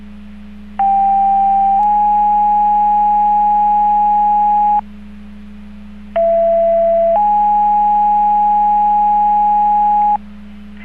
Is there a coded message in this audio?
Yes: two-tone page from 0.8 to 10.2 s, heard as a one-second tone, then a three-second tone, about 790 then 830 Hz, about 670 then 830 Hz.